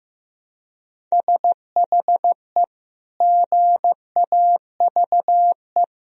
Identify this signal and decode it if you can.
Morse code, heard as a keyed tone, about 15 wpm, 700 Hz. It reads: SHE GAVE
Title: Sentences from Top 1000 Words - Review Part 1 - 15wpm